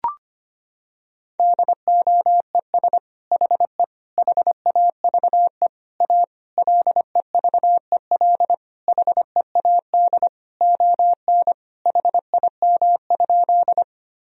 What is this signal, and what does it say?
Morse code, heard as a keyed tone, about 25 wpm, 700 Hz. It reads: DOES HE HAVE A LEVEL HEAD ON HIM?
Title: Sentences from Top 1000 Words - Review Part 4 - 25wpm